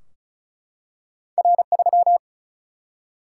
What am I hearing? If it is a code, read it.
Morse code, heard as a keyed tone, about 36 wpm, 700 Hz. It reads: R 3
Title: Single-Letter Single-Number 35wpm